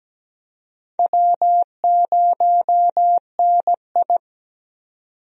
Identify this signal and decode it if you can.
Morse code, heard as a keyed tone, about 17 wpm, 700 Hz. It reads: W0NI
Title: USA Call Signs - 4 Characters - 17wpm